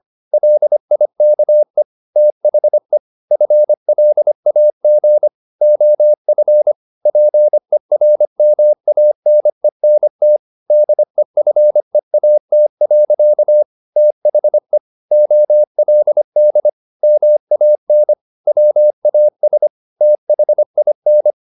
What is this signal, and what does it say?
Morse code, heard as a keyed tone, about 25 wpm, 600 Hz. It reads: LIKE THE FLAG OF PERMANENT DEFEAT. THE OLD MAN WAS THIN
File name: the old man and the sea part 1 25wpm